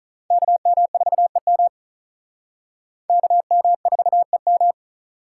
Morse code, heard as a keyed tone, about 42 wpm, 700 Hz. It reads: KM4EM KM4EM
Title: Speed Racing - USA 2x2 Call Signs 35wpm